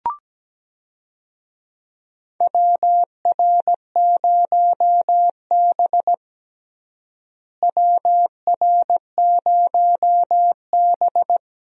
Morse code, heard as a keyed tone, about 17 wpm, 700 Hz. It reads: WR0B WR0B
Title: USA Call Signs - 4 Characters - Repeated 2x - 17wpm